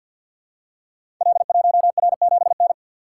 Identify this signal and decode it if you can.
Morse code, heard as a keyed tone, about 50 wpm, 700 Hz. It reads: L1R7N
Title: Code Groups - 5 Letter:Numbers - 50wpm